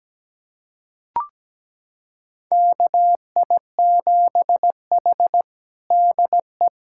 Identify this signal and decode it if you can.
Morse code, heard as a keyed tone, about 17 wpm, 700 Hz. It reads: KI7H DE
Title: USA Callsign DE Callsign - 4 Characters - 17wpm